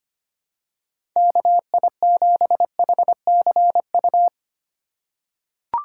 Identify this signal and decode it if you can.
Morse code, heard as a keyed tone, about 25 wpm, 700 Hz. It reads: KI7HCU
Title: USA Call Signs - 6 Characters - 2x Character Spacing - 25wpm